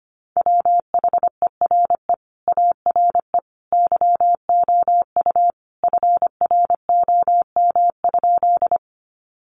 Morse code, heard as a keyed tone, about 25 wpm, 700 Hz. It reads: WHERE ARE YOU FROM?
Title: Sentences from Top 200 Words 25wpm